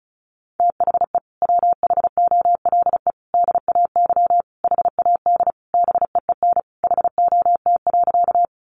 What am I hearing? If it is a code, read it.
Morse code, heard as a keyed tone, about 34 wpm, 700 Hz. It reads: THE WHOLE DAY HAD BEEN HOT.
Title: Sentences from Top 500 Words 35wpm